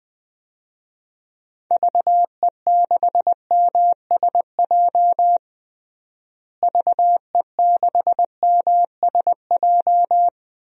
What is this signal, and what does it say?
Morse code, heard as a keyed tone, about 20 wpm, 700 Hz. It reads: VE6MSJ VE6MSJ
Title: DXCC-100 Call Signs - 6 Characters - Repeated 3x - 20wpm